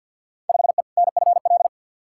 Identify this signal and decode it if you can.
Morse code, heard as a keyed tone, about 46 wpm, 700 Hz. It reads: 5E NFL
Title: Field Day Exchanges Encore - 50wpm